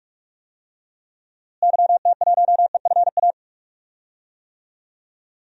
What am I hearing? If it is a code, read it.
Morse code, heard as a keyed tone, about 46 wpm, 700 Hz. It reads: YT1EFA
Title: DXCC-100 Call Signs - 6 Characters - Repeated 2x - 45wpm